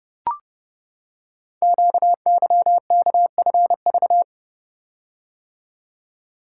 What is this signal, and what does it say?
Morse code, heard as a keyed tone, about 30 wpm, 700 Hz. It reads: QYKFV
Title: Code Groups - 5 Letters - 30wpm